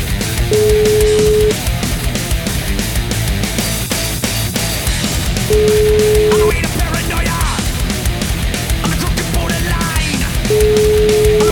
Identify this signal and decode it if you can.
Ringback tone (Europe, 425 Hz), with noise.